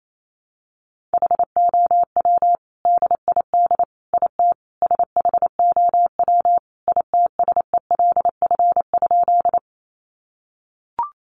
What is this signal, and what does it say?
Morse code, heard as a keyed tone, about 28 wpm, 700 Hz. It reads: HOW DID IT SHOW ITSELF?
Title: Sentences from Top 800 Words 28wpm